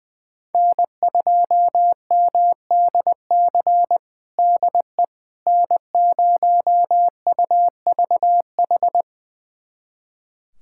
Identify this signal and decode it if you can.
Morse code, heard as a keyed tone, about 20 wpm, 700 Hz. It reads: N2MDC DE N0UVH
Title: USA Callsign DE Callsign - 5 Characters - 20wpm